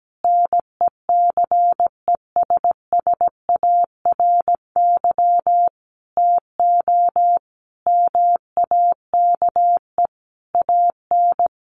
Morse code, heard as a keyed tone, about 17 wpm, 700 Hz. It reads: NECESSARY TO MAKE AN
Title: Sentences from Top 1200 Words 17wpm